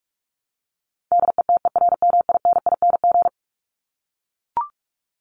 Morse code, heard as a keyed tone, about 44 wpm, 700 Hz. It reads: DETERMINING
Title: ING Suffix - Top 100 Words with Base Words 45wpm